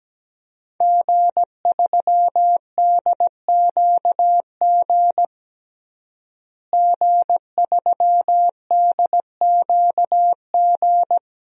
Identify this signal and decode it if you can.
Morse code, heard as a keyed tone, about 17 wpm, 700 Hz. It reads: G3DQG G3DQG
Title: DXCC-100 Call Signs - 5 Characters - Repeated 3x - 17wpm